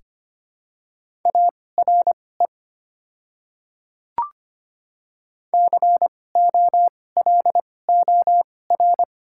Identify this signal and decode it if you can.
Morse code, heard as a keyed tone, about 25 wpm, spaced as for 18 wpm, 700 Hz. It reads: ARE COLOR
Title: 2x Character Spacing - Top 300 Words - 25wpm